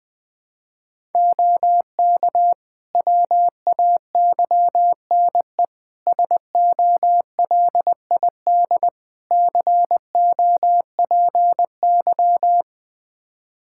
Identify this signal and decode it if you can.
Morse code, heard as a keyed tone, about 21 wpm, 700 Hz. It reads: OK WAYNE SOLID COPY
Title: Practice QSO - Exchange 3 - More Info - 20wpm